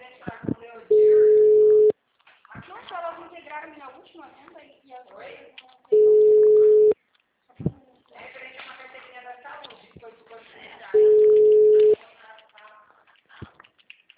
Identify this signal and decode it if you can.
Ringback tone (Europe, 425 Hz), with speech.
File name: out- -2007- - - .43382